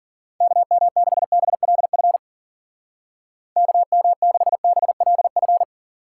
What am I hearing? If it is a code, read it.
Morse code, heard as a keyed tone, about 52 wpm, 700 Hz. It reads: KM6BLF KM6BLF